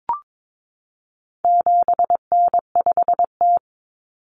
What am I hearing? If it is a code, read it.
Morse code, heard as a keyed tone, about 22 wpm, 700 Hz. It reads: 7N5T
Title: Common Cut Numbers - 4-Digit 22wpm